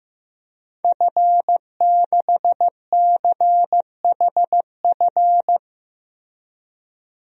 Morse code, heard as a keyed tone, about 15 wpm, 700 Hz. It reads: F6CHF
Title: DXCC-100 Call Signs - 5 Characters - Repeated 3x - 15wpm